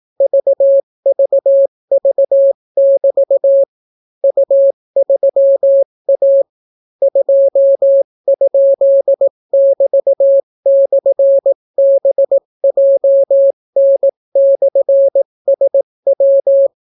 Morse code, heard as a keyed tone, about 18 wpm, 550 Hz. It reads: VVV<BT> U3A 2?<BT>/BJN/SW